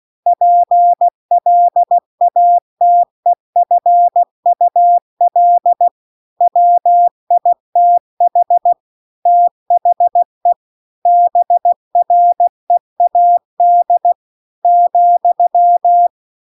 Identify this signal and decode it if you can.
Morse code, heard as a keyed tone, about 16 wpm, 700 Hz. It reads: PLATEFUL WITH THE BREAD ,